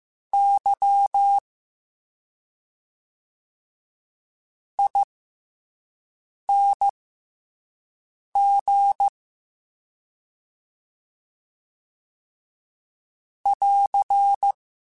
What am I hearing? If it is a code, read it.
Morse code, heard as a keyed tone, about 15 wpm, spaced as for 5 wpm, 800 Hz. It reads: Y ING <AR>